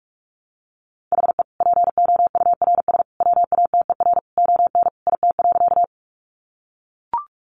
Morse code, heard as a keyed tone, about 44 wpm, 700 Hz. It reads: HE POURS WATER ON IT.
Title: Additional Easy Phrases 45wpm